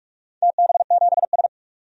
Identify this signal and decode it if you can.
Morse code, heard as a keyed tone, about 43 wpm, 700 Hz. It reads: TB7S
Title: DXCC-100 Call Signs - 4 Characters - 45wpm